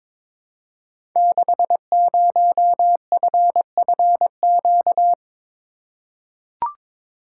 Morse code, heard as a keyed tone, about 22 wpm, 700 Hz. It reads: 60FFQ
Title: Code Groups - 5 Letter:Numbers - 22wpm